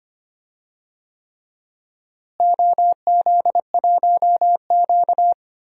Morse code, heard as a keyed tone, about 25 wpm, 700 Hz. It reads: OZ1Q
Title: DXCC-100 Call Signs - 4 Characters - Repeated 3x - 25wpm